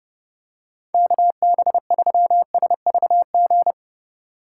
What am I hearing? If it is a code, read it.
Morse code, heard as a keyed tone, about 31 wpm, 700 Hz. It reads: KB3SVG